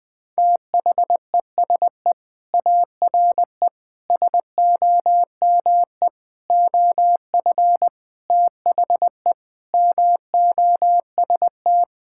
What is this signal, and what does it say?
Morse code, heard as a keyed tone, about 20 wpm, 700 Hz. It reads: THESE ARE SOME OF THE MOST
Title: Sentences with 4-Word Phrases - 20wpm